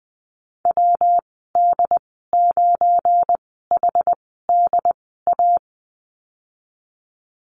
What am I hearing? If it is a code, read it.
Morse code, heard as a keyed tone, about 19 wpm, spaced as for 14 wpm, 700 Hz. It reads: WD9HDA
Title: 173 Lesson 28g - Number 9 - Call Signs 20wpm